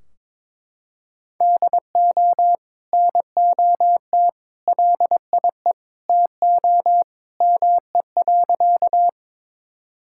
Morse code, heard as a keyed tone, about 22 wpm, 700 Hz. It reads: DO NOT LIE TO ME.
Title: Sentences from Top 1600 Words 22wpm